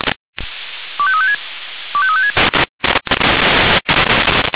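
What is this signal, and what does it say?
Five-tone selective call (ZVEI), heard as quick runs of five tones, tones of about 70 ms. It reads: 25356 25356
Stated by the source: ZVEI five-tone selective call received on an SDR.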